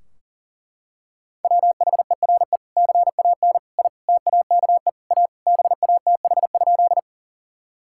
Morse code, heard as a keyed tone, about 40 wpm, 700 Hz. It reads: WHERE CAN I TAKE A BATH?
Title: Sentences from Top 1600 Words 40wpm